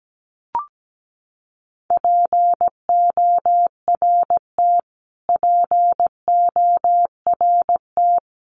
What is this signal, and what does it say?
Morse code, heard as a keyed tone, about 17 wpm, 700 Hz. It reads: PORT PORT
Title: Top 200 Words Repeated 17wpm